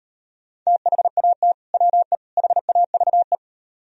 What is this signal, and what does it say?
Morse code, heard as a keyed tone, about 38 wpm, 700 Hz. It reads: THAT WE HAVE